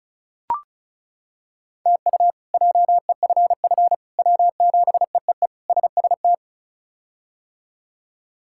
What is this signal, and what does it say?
Morse code, heard as a keyed tone, about 36 wpm, 700 Hz. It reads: TU JEFF W7EEE SST
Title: K1USN SST QSO Practice - 35wpm